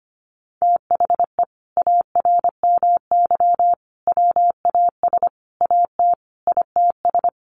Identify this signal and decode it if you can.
Morse code, heard as a keyed tone, about 25 wpm, 700 Hz. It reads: THE ARMY WAS AT ITS
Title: Sentences from Top 1300 Words 25wpm